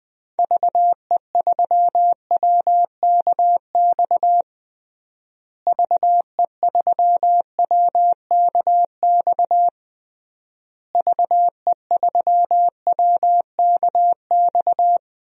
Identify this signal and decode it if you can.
Morse code, heard as a keyed tone, about 20 wpm, 700 Hz. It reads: VE3WKX VE3WKX VE3WKX
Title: DXCC-100 Call Signs - 6 Characters - Repeated 3x - 20wpm